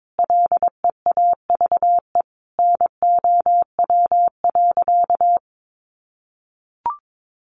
Morse code, heard as a keyed tone, about 22 wpm, 700 Hz. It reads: LEAVE NOW.
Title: Sentences from Top 1100 Words 22wpm